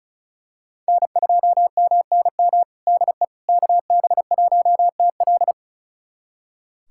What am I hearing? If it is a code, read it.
Morse code, heard as a keyed tone, about 36 wpm, 700 Hz. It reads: N2MNM DE KB1TL
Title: USA Callsign DE Callsign - 5 Characters - 35wpm